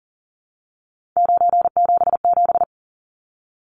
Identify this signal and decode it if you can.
Morse code, heard as a keyed tone, about 40 wpm, 700 Hz. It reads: 977
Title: Numbers - 3-Digit 40wpm